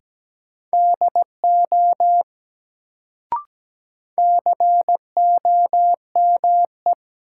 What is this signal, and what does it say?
Morse code, heard as a keyed tone, about 17 wpm, 700 Hz. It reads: DO COME
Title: Sets of 2 Words - New Word Drill - 17wpm